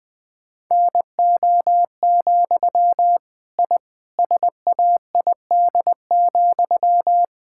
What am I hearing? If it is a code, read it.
Morse code, heard as a keyed tone, about 21 wpm, 700 Hz. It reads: NO, I SAID,
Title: Sentences from Top 1000 Words - Review Part 6 - 20wpm